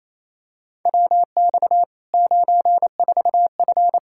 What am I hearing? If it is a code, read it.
Morse code, heard as a keyed tone, about 28 wpm, 700 Hz. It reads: WX 94F
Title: Practice QSO - Exchange 2 - Info - 28wpm